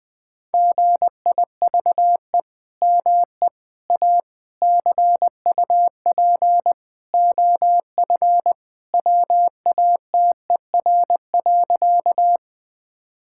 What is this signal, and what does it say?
Morse code, heard as a keyed tone, about 20 wpm, 700 Hz. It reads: GIVE ME A CUP OF WATER.
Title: Sentences from Top 1600 Words 20wpm